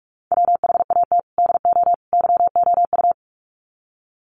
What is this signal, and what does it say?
Morse code, heard as a keyed tone, about 44 wpm, 700 Hz. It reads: WHAT DO YOU
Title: Sets of 3 Words 45wpm